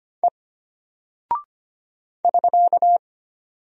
Morse code, heard as a keyed tone, about 25 wpm, 700 Hz. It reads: E <SK>
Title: Single Character (Letter:Number:Punctuation:Prosign) Alt 25wpm